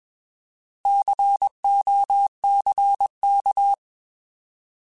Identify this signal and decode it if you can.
Morse code, heard as a keyed tone, about 20 wpm, 800 Hz. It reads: COCK